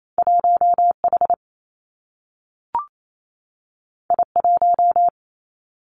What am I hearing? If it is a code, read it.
Morse code, heard as a keyed tone, about 28 wpm, 700 Hz. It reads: 1 H I 1
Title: Two Challenging Letter-Number Combinations 28wpm